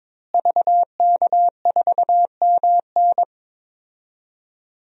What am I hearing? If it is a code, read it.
Morse code, heard as a keyed tone, about 22 wpm, 700 Hz. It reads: VK4MN
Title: DXCC-100 Call Signs - 5 Characters - Repeated 2x - 22wpm